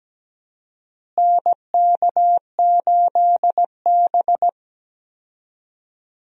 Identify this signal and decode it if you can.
Morse code, heard as a keyed tone, about 17 wpm, 700 Hz. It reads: NK8B
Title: USA Call Signs - 4 Characters - Repeated 2x - 17wpm